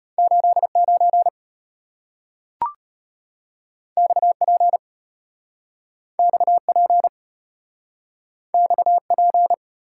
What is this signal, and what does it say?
Morse code, heard as a keyed tone, about 34 wpm, 690 Hz. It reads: 89 XP XP XP